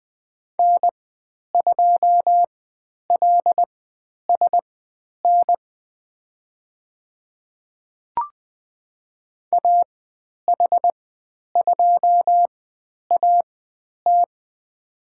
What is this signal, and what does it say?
Morse code, heard as a keyed tone, about 20 wpm, spaced as for 10 wpm, 700 Hz. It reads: N2LSN AH2AT
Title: 102 Lesson 17g - Letter u - Call Signs 20wpm